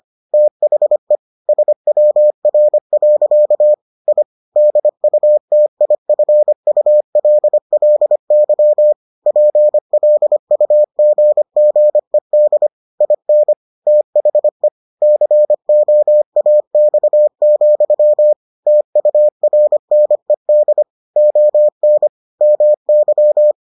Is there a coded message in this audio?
Yes: Morse code from 0.3 to 23.5 s, heard as a keyed tone, about 25 wpm, 600 Hz. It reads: THE SWR. I DUTIFULLY PLUGGED IN THE COAX, TURNED ON MY